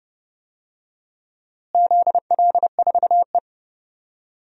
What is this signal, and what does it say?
Morse code, heard as a keyed tone, about 30 wpm, 700 Hz. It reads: ZL4E
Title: DXCC-100 Call Signs - 4 Characters - Repeated 2x - 30wpm